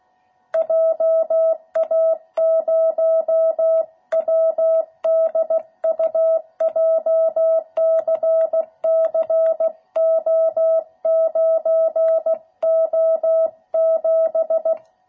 Morse code, heard as a keyed tone, about 16 wpm, 660 Hz. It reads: JA0WDU JCCO9O7